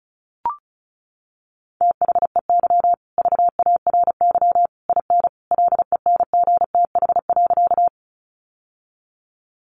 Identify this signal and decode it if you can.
Morse code, heard as a keyed tone, about 34 wpm, 700 Hz. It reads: THEY VARY IN LENGTH.
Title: Sentences from Top 1200 Words 35wpm